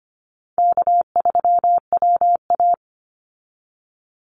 Morse code, heard as a keyed tone, about 25 wpm, 700 Hz. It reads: K3WA
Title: Easy Call Signs Repeated 2x - 25wpm